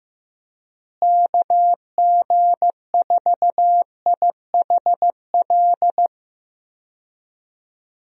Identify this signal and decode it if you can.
Morse code, heard as a keyed tone, about 15 wpm, 700 Hz. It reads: KG4IHL
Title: USA Call Signs - 6 Characters - Repeated 2x - 15wpm